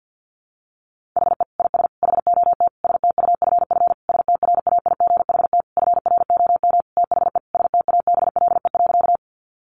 Morse code, heard as a keyed tone, about 50 wpm, 700 Hz. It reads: HE IS HOT STUFF STRAIGHT FROM THE STABLE.
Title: Sentences from Top 1500 Words 50wpm